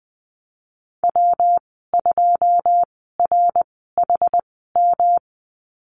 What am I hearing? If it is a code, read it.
Morse code, heard as a keyed tone, about 20 wpm, spaced as for 14 wpm, 700 Hz. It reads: W2RHM
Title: 193 Lesson 31g - Letter v - Call Signs 20wpm